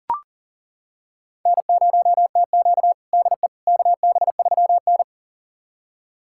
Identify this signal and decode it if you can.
Morse code, heard as a keyed tone, about 37 wpm, 700 Hz. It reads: N0TQ DE KB3N
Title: USA Callsign DE Callsign - 4 Characters - 40wpm